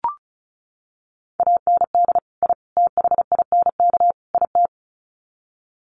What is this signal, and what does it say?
Morse code, heard as a keyed tone, about 34 wpm, 700 Hz. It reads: AND I THINK IT
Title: Sets of 4 Words 35wpm